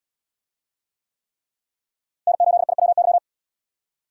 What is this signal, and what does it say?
Morse code, heard as a keyed tone, about 70 wpm, 700 Hz. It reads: N8IWY